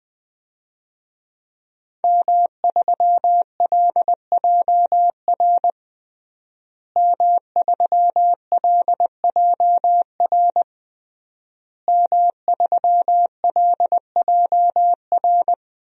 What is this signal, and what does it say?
Morse code, heard as a keyed tone, about 20 wpm, 700 Hz. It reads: M3LJR M3LJR M3LJR
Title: DXCC-100 Call Signs - 5 Characters - Repeated 3x - 20wpm